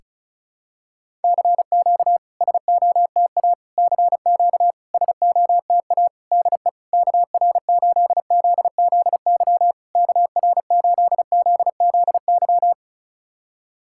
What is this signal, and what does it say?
Morse code, heard as a keyed tone, about 36 wpm, 700 Hz. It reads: CQ SOTA CQ SOTA DE KR8ZZY KR8ZZY
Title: SOTA QSO Practice 35wpm